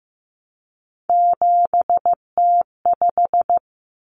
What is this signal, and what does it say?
Morse code, heard as a keyed tone, about 15 wpm, 700 Hz. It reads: 7T5
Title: Common Cut Numbers - 3-Digit 15wpm